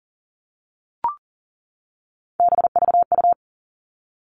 Morse code, heard as a keyed tone, about 40 wpm, 700 Hz. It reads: BVU
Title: All Cut Numbers - 3-Digit 40wpm